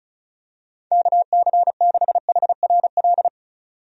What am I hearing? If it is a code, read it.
Morse code, heard as a keyed tone, about 34 wpm, 700 Hz. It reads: KC6HRL
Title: USA Call Signs - 6 Characters - 35wpm